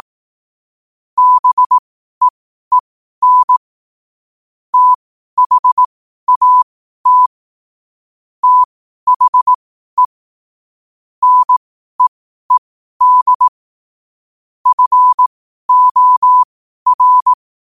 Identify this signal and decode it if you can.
Morse code, heard as a keyed tone, about 18 wpm, spaced as for 12 wpm, 1000 Hz. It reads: BEEN THAT THE NEED FOR
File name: qotdfiles / 12wpm